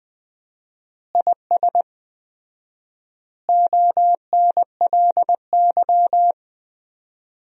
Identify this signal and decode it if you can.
Morse code, heard as a keyed tone, about 19 wpm, 700 Hz. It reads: IS ONLY
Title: 4x Word-Spacing - Sentences from Top 300 Words - 20wpm